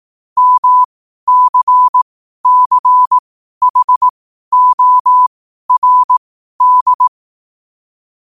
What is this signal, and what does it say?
Morse code, heard as a keyed tone, about 18 wpm, spaced as for 12 wpm, 1000 Hz. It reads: MCCHORD
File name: qotdfiles / 12wpm